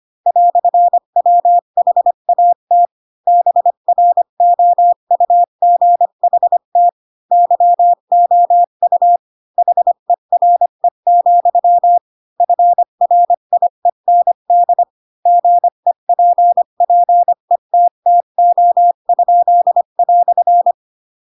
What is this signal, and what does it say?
Morse code, heard as a keyed tone, about 25 wpm, 700 Hz. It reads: "WHAT BROUGHT YOU HERE, FRIEND GEPPETTO?"